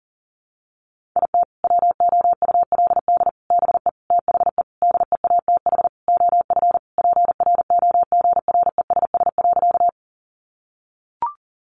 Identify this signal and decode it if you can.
Morse code, heard as a keyed tone, about 40 wpm, 700 Hz. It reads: IT WOULD BE THE DEATH OF PROGRESS.
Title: Sentences from Top 1400 Words 40wpm